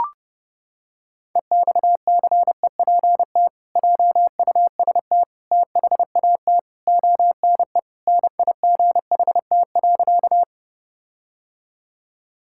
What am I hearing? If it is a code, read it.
Morse code, heard as a keyed tone, about 32 wpm, 700 Hz. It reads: EXCEPT JUST THAT ONE NIGHT.
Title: Sentences from Top 1600 Words 30wpm